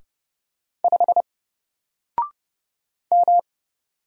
Morse code, heard as a keyed tone, about 31 wpm, 700 Hz. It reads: 5 M
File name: Single-Letter-Number-30wpm